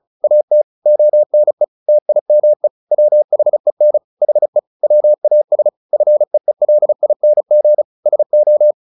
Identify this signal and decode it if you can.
Morse code, heard as a keyed tone, about 35 wpm, 600 Hz. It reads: AT ONE TIME WHEN HE WAS FEELING SO
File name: the old man and the sea 2 35wpm